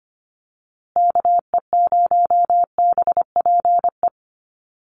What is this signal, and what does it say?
Morse code, heard as a keyed tone, about 25 wpm, 700 Hz. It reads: KE0BPE
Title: USA Call Signs Repeated 3x 25wpm